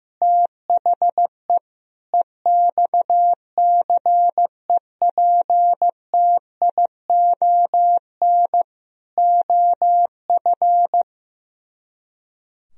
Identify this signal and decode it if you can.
Morse code, heard as a keyed tone, about 15 wpm, 700 Hz. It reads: THE EXCEPTION OF
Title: Speed-Racing - Sets of 4 Words 15wpm